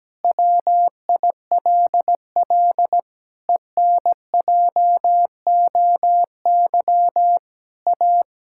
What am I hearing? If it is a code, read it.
Morse code, heard as a keyed tone, about 17 wpm, 700 Hz. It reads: WILL ENJOY A
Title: Sentences from Top 1000 Words - Review Part 5 - 17wpm